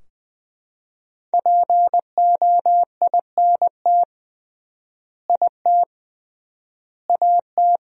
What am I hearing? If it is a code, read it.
Morse code, heard as a keyed tone, about 20 wpm, 700 Hz. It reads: POINT IT AT
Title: 3x Word-Spacing - Sentences from Top 300 Words - 20wpm